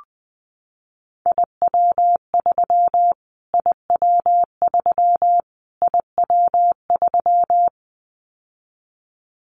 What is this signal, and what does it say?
Morse code, heard as a keyed tone, about 20 wpm, 700 Hz. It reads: IW3 IW3 IW3
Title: Callsign Prefixes for Top 25 DXCC - Repeated 3x - 20wpm